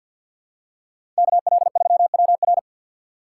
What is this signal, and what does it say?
Morse code, heard as a keyed tone, about 51 wpm, 700 Hz. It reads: KL3WR
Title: USA Call Signs - 5 Characters - 3x Character Spacing - 50wpm